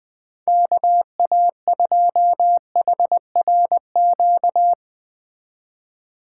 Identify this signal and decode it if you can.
Morse code, heard as a keyed tone, about 20 wpm, 700 Hz. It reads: KA2HRQ